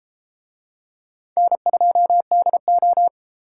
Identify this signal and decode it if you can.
Morse code, heard as a keyed tone, about 30 wpm, 700 Hz. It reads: N2DO